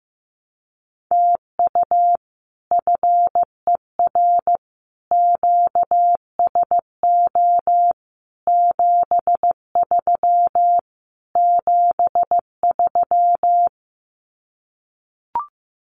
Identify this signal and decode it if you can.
Morse code, heard as a keyed tone, about 15 wpm, 700 Hz. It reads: TU FER QSO 73 73
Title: Common QSO Phrases Encore 15wpm